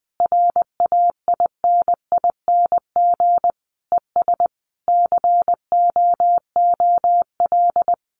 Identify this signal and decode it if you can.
Morse code, heard as a keyed tone, about 20 wpm, 700 Hz. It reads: RAINING ES COOL
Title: Common QSO Phrases 20wpm